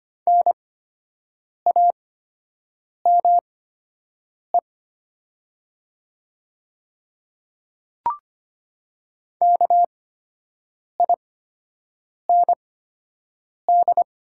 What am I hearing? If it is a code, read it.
Morse code, heard as a keyed tone, about 25 wpm, spaced as for 7 wpm, 700 Hz. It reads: NAME KIND
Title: 8x Character Spacing - Top 200 Words - 25wpm